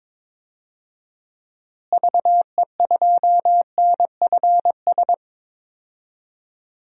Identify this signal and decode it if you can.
Morse code, heard as a keyed tone, about 21 wpm, 700 Hz. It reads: VE2NFS
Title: DXCC-100 Call Signs - 6 Characters - Repeated 2x - 22wpm